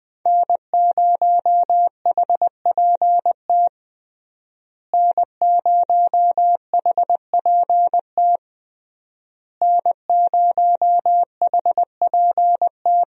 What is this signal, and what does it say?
Morse code, heard as a keyed tone, about 20 wpm, 700 Hz. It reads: N0HPT N0HPT N0HPT
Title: USA Call Signs - 5 Characters - Repeated 3x - 20wpm